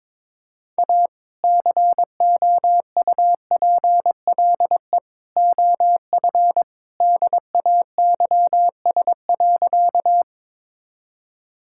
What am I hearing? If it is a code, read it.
Morse code, heard as a keyed tone, about 22 wpm, 700 Hz. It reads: A COUPLE OF DAYS.
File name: practice / 22wpm / Sentences with 2-Word Phrases - 22wpm